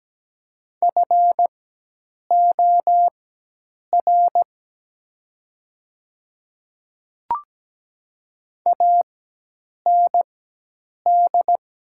Morse code, heard as a keyed tone, about 17 wpm, spaced as for 8 wpm, 700 Hz. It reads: FOR AND
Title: 4x Character Spacing - Top 100 Words - 17wpm